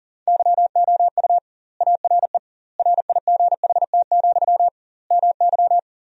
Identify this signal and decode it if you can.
Morse code, heard as a keyed tone, about 40 wpm, 700 Hz. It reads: YOU ARE RIGHT, MY
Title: 2x Word-Spacing - Sentences from Top 400 Words - 40wpm